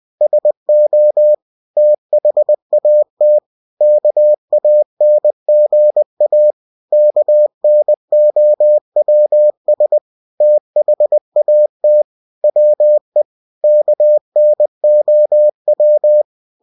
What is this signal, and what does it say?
Morse code, heard as a keyed tone, about 20 wpm, 600 Hz. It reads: SO THAT KANGA KNOWS THAT WE KNOW
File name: cw-books / winnie-the-pooh / winnie-20wpm-page-0061